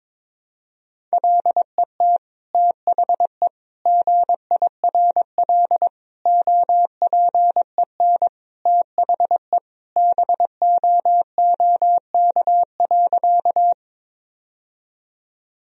Morse code, heard as a keyed tone, about 22 wpm, 700 Hz. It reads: LET THE GIRL OPEN THE BOOK.
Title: Sentences from Top 1000 Words - Review Part 1 - 22wpm